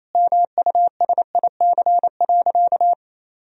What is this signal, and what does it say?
Morse code, heard as a keyed tone, about 28 wpm, 700 Hz. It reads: MUSIC.
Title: Sentences from Top 1000 Words - Review Part 1 - 28wpm